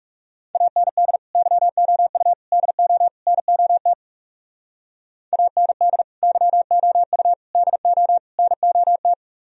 Morse code, heard as a keyed tone, about 36 wpm, 700 Hz. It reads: AND YOU DO NOT AND YOU DO NOT
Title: Speed-Racing - Sets of 4 Words 30wpm